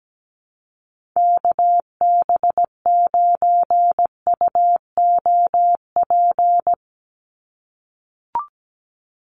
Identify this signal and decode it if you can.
Morse code, heard as a keyed tone, about 17 wpm, 700 Hz. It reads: KB9UOP
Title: USA Call Signs 17wpm